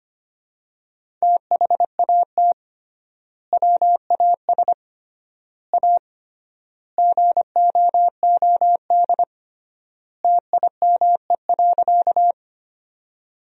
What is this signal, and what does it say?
Morse code, heard as a keyed tone, about 25 wpm, 700 Hz. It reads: THAT WAS A GOOD TIME.
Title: 3x Word-Spacing - Sentences from Top 200 Words - 25wpm